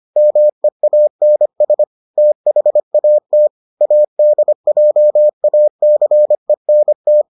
Morse code, heard as a keyed tone, about 25 wpm, 600 Hz. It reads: MEANS THAT ADJACENT